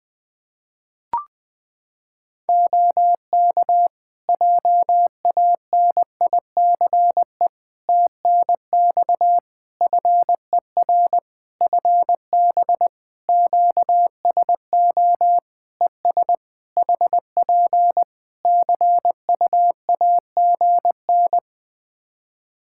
Morse code, heard as a keyed tone, about 20 wpm, 700 Hz. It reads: OK JANICE TNX FER FB QSO ES HP CUAGN
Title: K1USN SST QSO Practice 20wpm